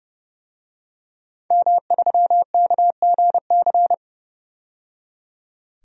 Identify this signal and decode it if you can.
Morse code, heard as a keyed tone, about 30 wpm, 700 Hz. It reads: M3KGC
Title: DXCC-100 Call Signs - 5 Characters - 30wpm